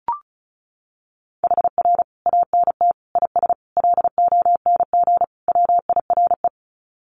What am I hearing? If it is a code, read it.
Morse code, heard as a keyed tone, about 34 wpm, 700 Hz. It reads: HR ANT IS LONG WIRE
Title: Common QSO Phrases 35wpm